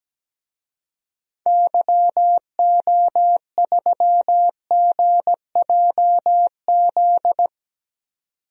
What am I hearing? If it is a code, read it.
Morse code, heard as a keyed tone, about 17 wpm, 700 Hz. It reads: YO3GJZ
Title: DXCC-100 Call Signs - 6 Characters - 17wpm